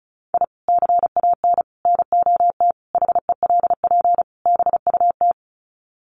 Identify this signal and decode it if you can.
Morse code, heard as a keyed tone, about 34 wpm, 700 Hz. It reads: I CAN NOT HELP BUT
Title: Sets of 5 Words Encore2 35wpm